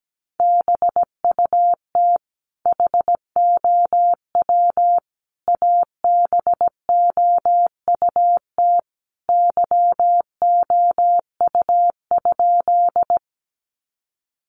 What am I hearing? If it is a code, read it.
Morse code, heard as a keyed tone, about 17 wpm, 700 Hz. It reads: BUT HOW ABOUT YOU?
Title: Sentences from Top 100 Words 17wpm